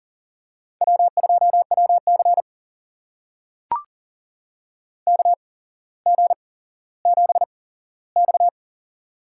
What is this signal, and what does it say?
Morse code, heard as a keyed tone, about 40 wpm, spaced as for 17 wpm, 700 Hz. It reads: W2WC KG7X
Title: USA Call Signs - 4 Characters - 8x Character Spacing - 40wpm